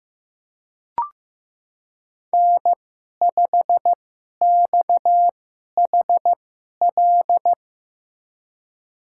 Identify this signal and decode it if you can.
Morse code, heard as a keyed tone, about 15 wpm, spaced as for 11 wpm, 700 Hz. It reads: N5XHL
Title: USA Call Signs - 5 Characters - 2x Character Spacing - 15wpm